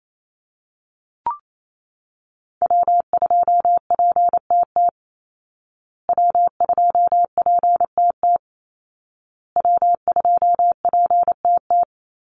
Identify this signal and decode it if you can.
Morse code, heard as a keyed tone, about 28 wpm, 700 Hz. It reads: W2PTT W2PTT W2PTT
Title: Call Signs Repeated 2x - 28wpm